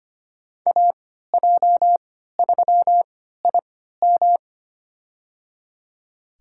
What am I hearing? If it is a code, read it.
Morse code, heard as a keyed tone, about 25 wpm, spaced as for 14 wpm, 700 Hz. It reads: AJ3IM
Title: USA Call Signs - 5 Characters - 3x Character Spacing - 25wpm